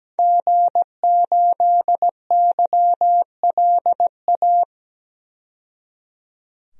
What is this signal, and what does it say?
Morse code, heard as a keyed tone, about 17 wpm, 700 Hz. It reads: G8YLA
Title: DXCC-100 Call Signs - 5 Characters - 17wpm